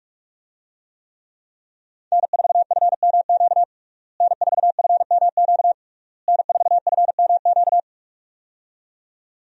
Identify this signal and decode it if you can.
Morse code, heard as a keyed tone, about 45 wpm, 700 Hz. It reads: N4FMQ N4FMQ N4FMQ
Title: USA Call Signs - 5 Characters - Repeated 3x - 45wpm